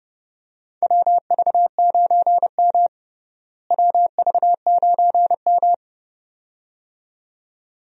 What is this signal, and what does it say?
Morse code, heard as a keyed tone, about 30 wpm, 700 Hz. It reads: WV9M WV9M
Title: USA Call Signs - 4 Characters - Repeated 2x - 30wpm